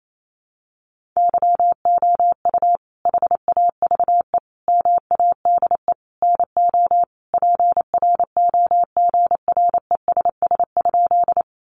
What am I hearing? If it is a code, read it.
Morse code, heard as a keyed tone, about 28 wpm, 700 Hz. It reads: YOU HAVE MADE NO PROGRESS?
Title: Sentences from Top 1400 Words 28wpm